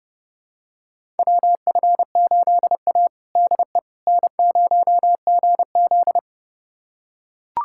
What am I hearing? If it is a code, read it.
Morse code, heard as a keyed tone, about 30 wpm, 700 Hz. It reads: WF8A DE N0GZ
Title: USA Callsign DE Callsign - 4 Characters - 30wpm